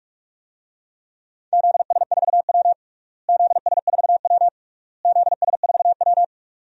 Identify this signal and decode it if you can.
Morse code, heard as a keyed tone, about 44 wpm, 700 Hz. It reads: ZS4W ZS4W ZS4W
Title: DXCC-100 Call Signs - 4 Characters - Repeated 3x - 45wpm